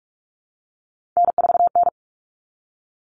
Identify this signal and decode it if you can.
Morse code, heard as a keyed tone, about 44 wpm, 700 Hz. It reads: N4N
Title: Common Cut Numbers - 3-Digit 45wpm